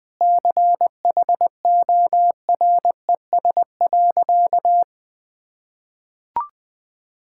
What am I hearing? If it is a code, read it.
Morse code, heard as a keyed tone, about 20 wpm, 700 Hz. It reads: CHORES.
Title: Sentences with 5-Word Phrases - 20wpm